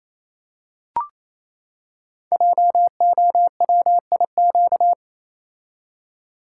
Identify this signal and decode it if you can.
Morse code, heard as a keyed tone, about 27 wpm, 700 Hz. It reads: JOWIQ
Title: Code Groups - 5 Letters - 28wpm